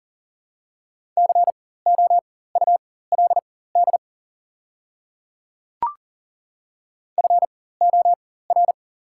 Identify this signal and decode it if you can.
Morse code, heard as a keyed tone, about 40 wpm, spaced as for 19 wpm, 690 Hz. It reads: COULD FOR
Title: 4x Character Spacing - Top 100 Words - 40wpm